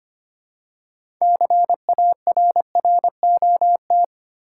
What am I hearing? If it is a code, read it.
Morse code, heard as a keyed tone, about 25 wpm, 700 Hz. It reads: CARROT
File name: practice / 25wpm / Easy Words with Hints - 25wpm